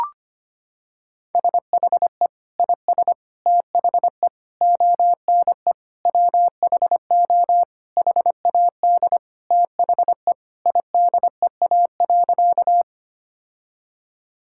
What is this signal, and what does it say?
Morse code, heard as a keyed tone, about 25 wpm, 700 Hz. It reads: SHE IS THE ONE WHO HAD THE IDEA.
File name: practice / 25wpm / Sentences with 2-Word Phrases - 25wpm